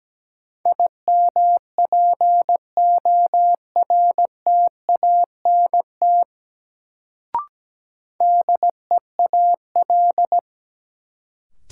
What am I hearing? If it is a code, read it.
Morse code, heard as a keyed tone, about 17 wpm, 700 Hz. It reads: IMPORTANT DEAL
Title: Sets of 3 Words - New Word Drill - 17wpm